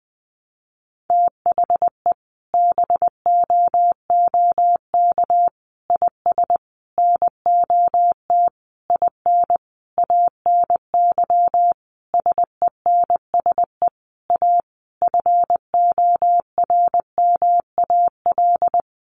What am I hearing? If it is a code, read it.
Morse code, heard as a keyed tone, about 20 wpm, 700 Hz. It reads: THE BOOK IS NOT IN ANY SENSE A FORMAL
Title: Sentences from Top 1300 Words 20wpm